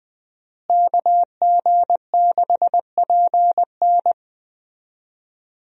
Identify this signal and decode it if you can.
Morse code, heard as a keyed tone, about 20 wpm, 700 Hz. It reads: KG6PN
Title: USA Call Signs - 5 Characters - Repeated 3x - 20wpm